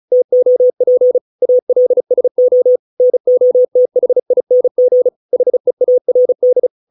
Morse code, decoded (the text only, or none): TOP ALSO NOTHING HEARD